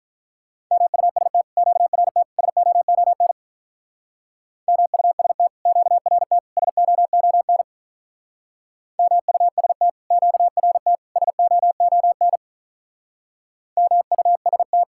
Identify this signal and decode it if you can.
Morse code, heard as a keyed tone, about 43 wpm, 700 Hz. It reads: MUST QRT SOON MUST QRT SOON MUST QRT SOON MUST
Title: Speed-Racing Common QSO Phrases - 35wpm